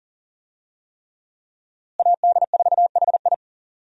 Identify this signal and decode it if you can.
Morse code, heard as a keyed tone, about 40 wpm, 700 Hz. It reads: AD4HI
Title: USA Call Signs - 5 Characters - Repeated 3x - 40wpm